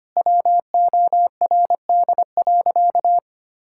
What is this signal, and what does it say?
Morse code, heard as a keyed tone, about 25 wpm, 700 Hz. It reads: WORD.
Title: Pangram Challenge v2 - 25wpm